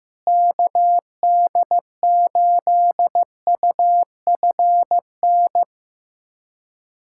Morse code, heard as a keyed tone, about 15 wpm, 700 Hz. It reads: KD8UFN
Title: USA Callsign DE Callsign - 6 Characters - 15wpm